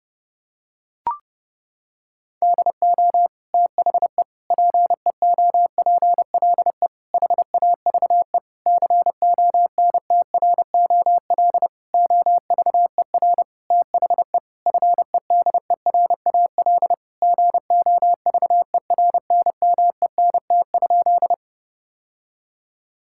Morse code, heard as a keyed tone, about 32 wpm, 700 Hz. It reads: DO THE PEOPLE HAVE CONTROL OVER THE FEDERAL GOVERNMENT?
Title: Sentences from Top 1000 Words - Review Part 5 - 30wpm